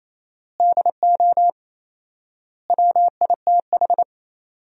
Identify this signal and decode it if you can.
Morse code, heard as a keyed tone, about 28 wpm, 700 Hz. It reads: DO WITH
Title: 4x Word-Spacing - Sentences from Top 200 Words - 28wpm